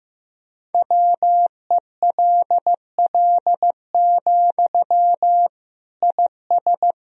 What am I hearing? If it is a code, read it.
Morse code, heard as a keyed tone, about 15 wpm, 700 Hz. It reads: WELL, IS
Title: Sentences from Top 1000 Words - Review Part 1 - 15wpm